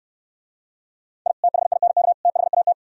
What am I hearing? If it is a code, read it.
Morse code, heard as a keyed tone, about 64 wpm, 700 Hz. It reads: I THINK THAT